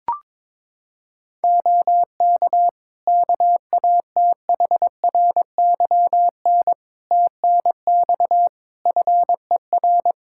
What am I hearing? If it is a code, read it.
Morse code, heard as a keyed tone, about 22 wpm, 700 Hz. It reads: OK KATHRYN TNX FER
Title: K1USN SST QSO Practice 22wpm